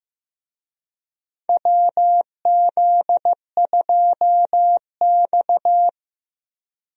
Morse code, heard as a keyed tone, about 15 wpm, 700 Hz. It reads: WZ2X
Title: USA Call Signs - 4 Characters - Repeated 2x - 15wpm